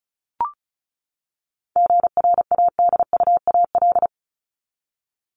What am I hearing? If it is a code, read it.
Morse code, heard as a keyed tone, about 34 wpm, 700 Hz. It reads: GRADUAL
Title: LY Suffix - Top 100 Words with Base Words 35wpm